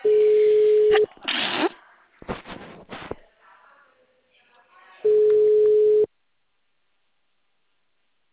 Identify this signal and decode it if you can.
Ringback tone (Europe, 425 Hz), with speech.